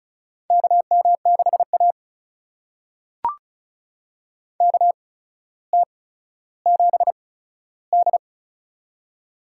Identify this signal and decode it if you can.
Morse code, heard as a keyed tone, about 34 wpm, spaced as for 15 wpm, 700 Hz. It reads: KM6A KT7D
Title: USA Call Signs - 4 Characters - 8x Character Spacing - 35wpm